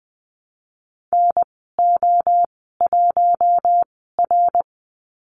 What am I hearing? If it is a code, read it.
Morse code, heard as a keyed tone, about 20 wpm, spaced as for 15 wpm, 700 Hz. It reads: NO1R
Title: 062 Lesson 11g - Letter h - Call Signs 20wpm